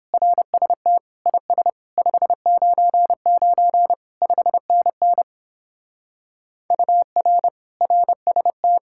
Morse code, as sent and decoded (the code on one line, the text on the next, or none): .-. ... - / .. ... / ..... ----. ----. / ..... -. -. / ..- .-. / .-. ... -
RST IS 599 5NN UR RST